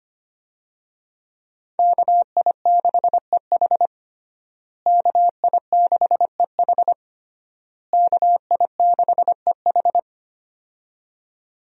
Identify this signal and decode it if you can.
Morse code, heard as a keyed tone, about 25 wpm, 700 Hz. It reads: KI6EH KI6EH KI6EH
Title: USA Call Signs - 5 Characters - Repeated 3x - 25wpm